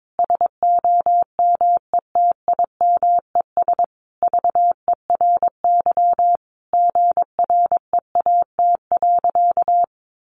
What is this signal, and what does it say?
Morse code, heard as a keyed tone, about 22 wpm, 700 Hz. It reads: SOMETIMES VERY GREAT.
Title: Sentences from Top 1200 Words 22wpm